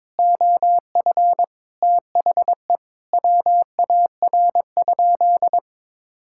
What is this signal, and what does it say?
Morse code, heard as a keyed tone, about 22 wpm, 700 Hz. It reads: OF THE WAR?
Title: Sentences from Top 1000 Words - Review Part 6 - 22wpm